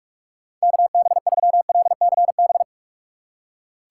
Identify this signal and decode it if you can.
Morse code, heard as a keyed tone, about 42 wpm, 700 Hz. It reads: KB3LCB
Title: USA Call Signs - 6 Characters - 2x Character Spacing - 45wpm